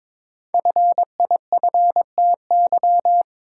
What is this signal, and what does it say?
Morse code, heard as a keyed tone, about 21 wpm, 700 Hz. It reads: FIFTY
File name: practice / 22wpm / Sentences with 3-Word Phrases - 22wpm